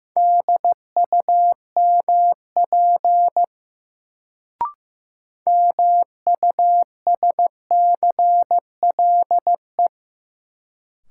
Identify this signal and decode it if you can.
Morse code, heard as a keyed tone, about 15 wpm, 700 Hz. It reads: DUMP MUSCLE
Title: Sentences from Top 1300 Words - New Word Drill - 15wpm